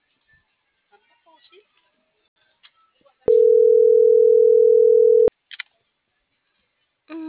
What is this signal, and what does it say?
Ringback tone (North America), 440 + 480 Hz.